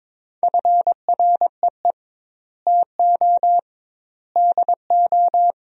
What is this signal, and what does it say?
Morse code, heard as a keyed tone, about 23 wpm, 700 Hz. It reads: FREE TO DO
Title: 2x Word-Spacing - Sentences from Top 500 Words - 22wpm